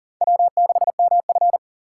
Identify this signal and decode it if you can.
Morse code, heard as a keyed tone, about 38 wpm, 700 Hz. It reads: W6MF
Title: USA Call Signs - 4 Characters - 3x Character Spacing - 40wpm